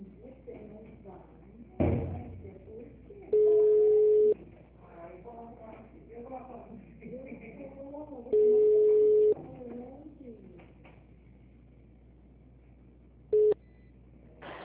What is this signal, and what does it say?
Ringback tone (Europe, 425 Hz), with speech.
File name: out- -2007- - - .238